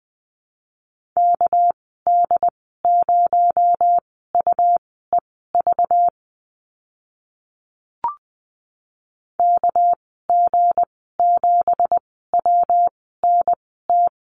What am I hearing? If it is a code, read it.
Morse code, heard as a keyed tone, about 20 wpm, spaced as for 14 wpm, 700 Hz. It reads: KD0UEV KG7WNT
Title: 231 Lesson 37g - Letter x - Call Signs 20wpm